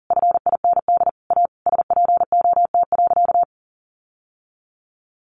Morse code, heard as a keyed tone, about 40 wpm, 700 Hz. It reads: FIND A SPOT.
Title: Sentences from Top 1100 Words 40wpm